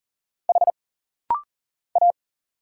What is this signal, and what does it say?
Morse code, heard as a keyed tone, about 40 wpm, 700 Hz. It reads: H A
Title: Single Letter-Number Alt - Rapid-Fire 40wpm